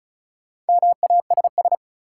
Morse code, heard as a keyed tone, about 36 wpm, 700 Hz. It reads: MASS
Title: Sets of 5 Words Encore - New Word Drill - 35wpm